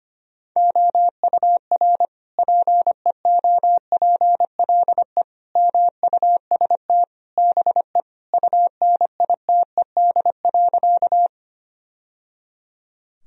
Morse code, heard as a keyed tone, about 25 wpm, 700 Hz. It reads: OUR PEOPLE MUST BE UNITED.
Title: Sentences from Top 1000 Words - Review Part 4 - 25wpm